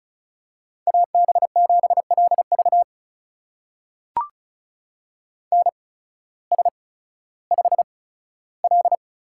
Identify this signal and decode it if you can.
Morse code, heard as a keyed tone, about 36 wpm, spaced as for 16 wpm, 700 Hz. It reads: AB7LV NS5L
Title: USA Call Signs - 8x Character Spacing - 35wpm